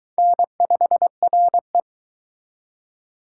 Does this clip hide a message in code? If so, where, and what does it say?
Yes: Morse code from 0.2 to 1.8 s, heard as a keyed tone, about 23 wpm, 700 Hz. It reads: N5RE